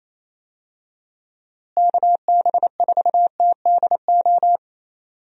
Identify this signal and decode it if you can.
Morse code, heard as a keyed tone, about 27 wpm, 700 Hz. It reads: KB4TDO
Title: USA Call Signs - 6 Characters - Repeated 3x - 28wpm